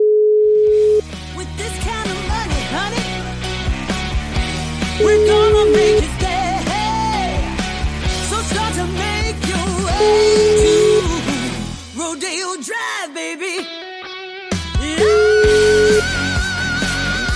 Ringback tone (Europe, 425 Hz), with noise.